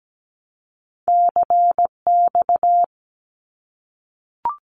Morse code, heard as a keyed tone, about 17 wpm, 700 Hz. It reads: CX
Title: Abbreviations 17wpm